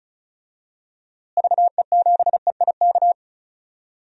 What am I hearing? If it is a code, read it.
Morse code, heard as a keyed tone, about 36 wpm, 700 Hz. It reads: VE7EIK
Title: DXCC-100 Call Signs - 6 Characters - Repeated 3x - 35wpm